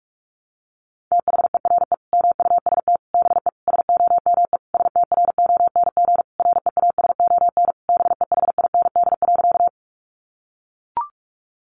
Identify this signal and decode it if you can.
Morse code, heard as a keyed tone, about 44 wpm, 700 Hz. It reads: THERE MUST BE SOME STRONG REASON BEHIND.
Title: Sentences from Top 700 Words 45wpm